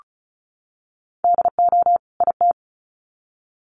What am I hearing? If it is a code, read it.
Morse code, heard as a keyed tone, about 34 wpm, 700 Hz. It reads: DO IT
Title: Sets of 2 Words 35wpm